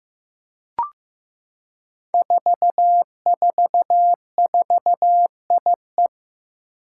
Morse code, heard as a keyed tone, about 15 wpm, 700 Hz. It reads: 444IE
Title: Five Challenging Letter-Number Combinations 15wpm